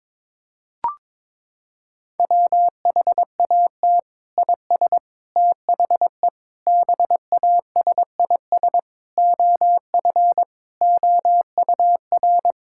Morse code, heard as a keyed tone, about 22 wpm, 700 Hz. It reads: WHAT IS THE BASIS OF OUR
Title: Sentences from Top 1000 Words - Review Part 5 - 22wpm